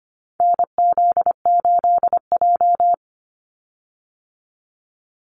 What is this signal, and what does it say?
Morse code, heard as a keyed tone, about 25 wpm, 700 Hz. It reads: NZ8J
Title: Call Signs - 25wpm